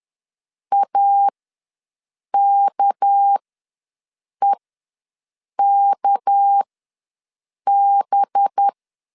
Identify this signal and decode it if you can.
Morse code, heard as a keyed tone, about 10 wpm, spaced as for 6 wpm, 800 Hz. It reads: AKEKB
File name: LEZIONI / 6 PKX0 / 6 PK ws 10wpm 3745